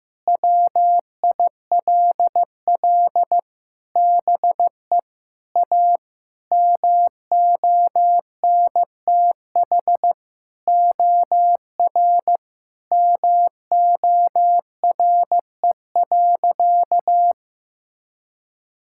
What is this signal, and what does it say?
Morse code, heard as a keyed tone, about 14 wpm, 700 Hz. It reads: WILL BE A MONTH OR MORE.
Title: Sentences from Top 1000 Words - Review Part 5 - 15wpm